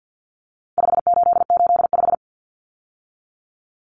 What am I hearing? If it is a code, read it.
Morse code, heard as a keyed tone, about 48 wpm, 700 Hz. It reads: 5885
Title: Numbers - 4-Digit 50wpm